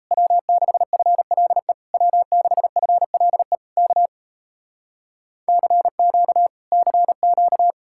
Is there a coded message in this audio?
Yes: Morse code from 0.1 to 7.7 s, heard as a keyed tone, about 37 wpm, 700 Hz. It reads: W6FLE W6FLE K CQ CQ